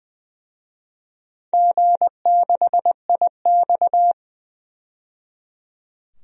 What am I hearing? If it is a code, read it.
Morse code, heard as a keyed tone, about 21 wpm, 700 Hz. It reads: G6IX
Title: DXCC-100 Call Signs - 4 Characters - 20wpm